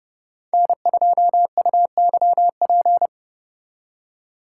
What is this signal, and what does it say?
Morse code, heard as a keyed tone, about 31 wpm, 700 Hz. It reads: N2UYP
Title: USA Call Signs - 5 Characters - 2x Character Spacing - 30wpm